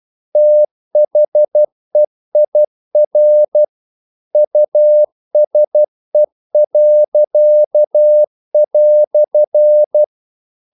Morse code, heard as a keyed tone, about 12 wpm, 600 Hz. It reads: THEIR USE."
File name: CW / Practice / EnglishQuotations0112